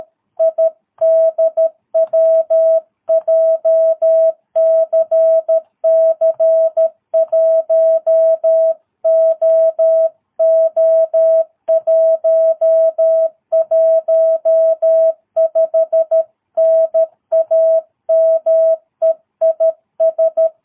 Morse code, heard as a keyed tone, about 13 wpm, 650 Hz. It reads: IDW JCC1OO115 NAME IS
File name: jh8cbh / 2022cw